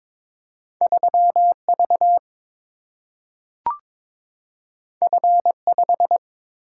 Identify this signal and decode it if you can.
Morse code, heard as a keyed tone, about 22 wpm, 700 Hz. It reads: 3 V F 5
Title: Single-Letter Single-Number 22wpm